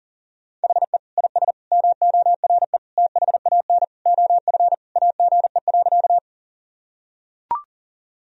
Morse code, heard as a keyed tone, about 40 wpm, 700 Hz. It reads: HE IS MORE THAN OF AGE.
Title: Sentences from Top 1000 Words - Review Part 3 - 40wpm